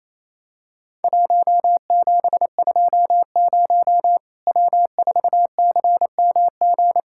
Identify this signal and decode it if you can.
Morse code, heard as a keyed tone, about 28 wpm, 700 Hz. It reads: 1720 W4CMG
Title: Worked All Europe DX Contest - QTC Exchange - 28wpm